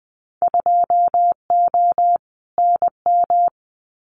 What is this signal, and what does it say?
Morse code, heard as a keyed tone, about 20 wpm, 700 Hz. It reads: 2O NM
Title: Winter Field Day Exchanges 20wpm